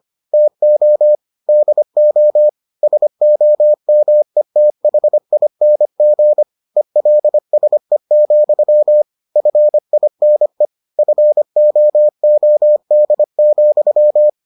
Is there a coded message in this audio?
Yes: Morse code from 0.3 to 14.3 s, heard as a keyed tone, about 25 wpm, 600 Hz. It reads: TO DO SOMETHING ELSE, FINE FOOD,